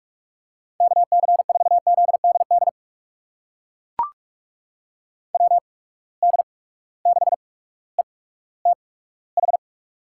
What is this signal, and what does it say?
Morse code, heard as a keyed tone, about 42 wpm, spaced as for 19 wpm, 700 Hz. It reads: KC4ZDD WD6ETH
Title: USA Call Signs - 6 Characters - 8x Character Spacing - 45wpm